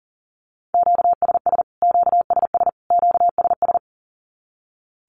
Morse code, heard as a keyed tone, about 40 wpm, 700 Hz. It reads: QSS QSS QSS
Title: Q-Codes Repeated 3x 40wpm